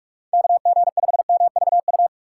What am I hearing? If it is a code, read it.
Morse code, heard as a keyed tone, about 42 wpm, 700 Hz. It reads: KG5MVU
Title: USA Call Signs - 4x Character Spacing - 45wpm